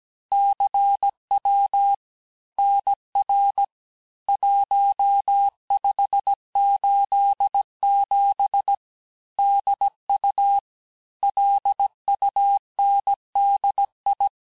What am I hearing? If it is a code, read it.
Morse code, heard as a keyed tone, about 17 wpm, 800 Hz. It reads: CW NR 1587 DU LUNDI
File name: cours-du-lundi files / cours-1587-texte-chiffre-vitesse-16MPM